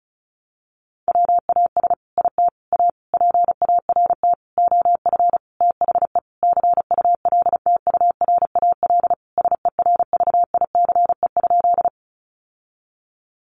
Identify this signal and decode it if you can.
Morse code, heard as a keyed tone, about 34 wpm, 700 Hz. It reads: WAS IT A PART OF THE CULTURAL SERVICE?
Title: Sentences from Top 1500 Words 35wpm